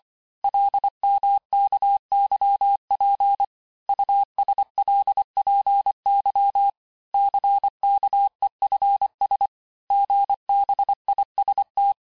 Morse code, heard as a keyed tone, about 25 wpm, 800 Hz. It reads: LMKYP USLPY CKEFS GBIST
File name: texte-chiffre-vitesse-23mpm